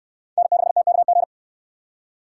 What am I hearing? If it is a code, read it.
Morse code, heard as a keyed tone, about 67 wpm, 700 Hz. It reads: N6TKG